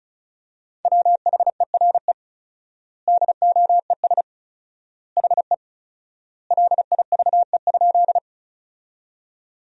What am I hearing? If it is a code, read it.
Morse code, heard as a keyed tone, about 36 wpm, 700 Hz. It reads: WHERE DOES HE LIVE?
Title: 4x Word-Spacing - Sentences from Top 200 Words - 35wpm